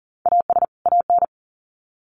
Morse code, heard as a keyed tone, about 40 wpm, 700 Hz. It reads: AS AN
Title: Sets of 2 Words 40wpm